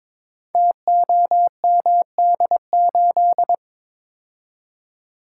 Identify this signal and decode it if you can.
Morse code, heard as a keyed tone, about 22 wpm, 700 Hz. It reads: TOMD8
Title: Code Groups - 5 Letter:Numbers - 22wpm